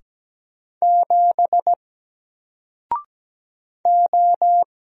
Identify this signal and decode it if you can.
Morse code, heard as a keyed tone, about 17 wpm, 700 Hz. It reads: 7 O
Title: Single Letter-Number Alt 17wpm